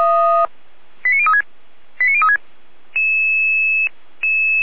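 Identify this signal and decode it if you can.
Five-tone selective call (ZVEI), heard as quick runs of five tones, tones of about 70 ms. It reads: 89036 89036
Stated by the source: ZVEI five-tone selective call received on an SDR.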